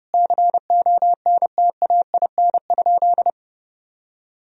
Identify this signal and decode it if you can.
Morse code, heard as a keyed tone, about 30 wpm, 700 Hz. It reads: CONTAIN?
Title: Sentences from Top 1000 Words - Review Part 2 - 30wpm